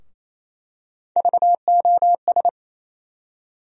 Morse code, heard as a keyed tone, about 28 wpm, 700 Hz. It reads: VOS
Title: USA Call Sign Postfixes - 3 Letters 28wpm